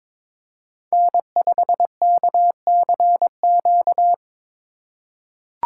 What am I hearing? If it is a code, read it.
Morse code, heard as a keyed tone, about 22 wpm, 700 Hz. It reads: N5KCQ
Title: USA Call Signs - 5 Characters - 3x Character Spacing - 22wpm